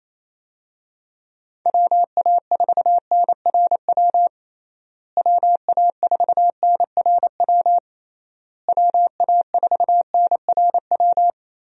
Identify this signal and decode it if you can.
Morse code, heard as a keyed tone, about 28 wpm, 700 Hz. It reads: WA4NRW WA4NRW WA4NRW
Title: USA Call Signs - 6 Characters - Repeated 3x - 28wpm